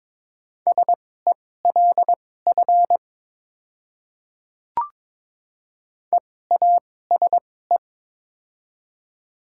Morse code, heard as a keyed tone, about 22 wpm, spaced as for 16 wpm, 700 Hz. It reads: SELF EASE
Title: 2x Character Spacing - Top 300 Words - 22wpm